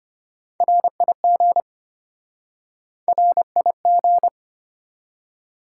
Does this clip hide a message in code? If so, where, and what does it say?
Yes: Morse code from 0.6 to 4.3 s, heard as a keyed tone, about 32 wpm, 700 Hz. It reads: RIG RIG